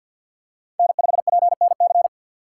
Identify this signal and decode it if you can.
Morse code, heard as a keyed tone, about 48 wpm, 690 Hz. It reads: N5PNC
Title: USA Call Signs - 5 Characters - 4x Character Spacing - 50wpm